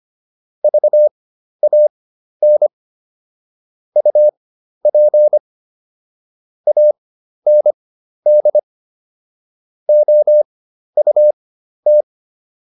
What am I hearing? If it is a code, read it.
Morse code, heard as a keyed tone, about 25 wpm, spaced as for 12 wpm, 600 Hz. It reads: VAN UP AND OUT